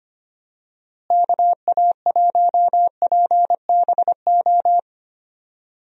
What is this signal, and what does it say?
Morse code, heard as a keyed tone, about 25 wpm, 700 Hz. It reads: KA1PBO
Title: USA Call Signs - 6 Characters - Repeated 2x - 25wpm